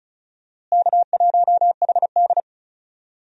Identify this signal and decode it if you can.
Morse code, heard as a keyed tone, about 36 wpm, 700 Hz. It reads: K1HD
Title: USA Call Signs - 4 Characters - Repeated 3x - 35wpm